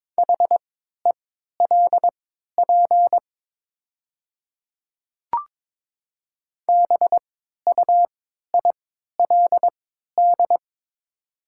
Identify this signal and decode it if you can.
Morse code, heard as a keyed tone, about 22 wpm, spaced as for 12 wpm, 700 Hz. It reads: HELP BUILD
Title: 3x Character Spacing - Top 200 Words - 22wpm